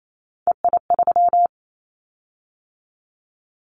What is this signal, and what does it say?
Morse code, heard as a keyed tone, about 28 wpm, 700 Hz. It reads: EI3
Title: Callsign Prefixes for Top 50 DXCC - 28wpm